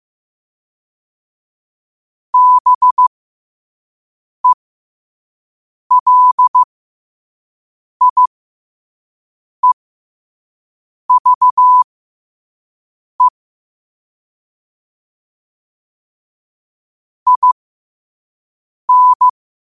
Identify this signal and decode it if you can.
Morse code, heard as a keyed tone, about 15 wpm, spaced as for 5 wpm, 1000 Hz. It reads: BELIEVE IN
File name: qotdfiles / 5wpm